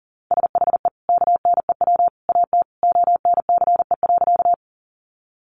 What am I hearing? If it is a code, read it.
Morse code, heard as a keyed tone, about 40 wpm, 700 Hz. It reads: SHE KNEW AT ONCE.
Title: Sentences from Top 400 Words 40wpm